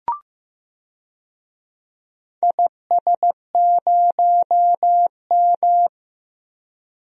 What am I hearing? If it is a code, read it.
Morse code, heard as a keyed tone, about 15 wpm, 700 Hz. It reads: IS0M
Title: DXCC-100 Call Signs - 4 Characters - 15wpm